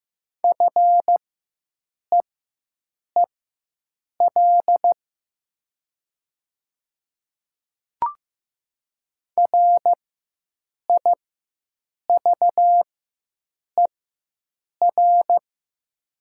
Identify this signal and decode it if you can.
Morse code, heard as a keyed tone, about 15 wpm, spaced as for 7 wpm, 700 Hz. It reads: FEEL RIVER
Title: 4x Character Spacing - Top 500 Words - 15wpm